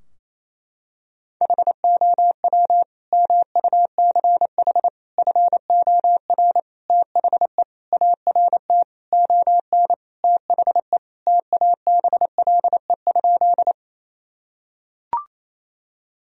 Morse code, heard as a keyed tone, about 28 wpm, 700 Hz. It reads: HOW MUCH FOR THE ART ON THE TABLE?
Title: Sentences from Top 1000 Words - Review Part 3 - 28wpm